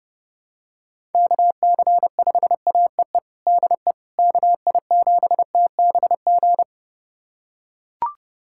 Morse code, heard as a keyed tone, about 30 wpm, 700 Hz. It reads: KC5AEE DE KI7TBG
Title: USA Callsign DE Callsign - 6 Characters - 30wpm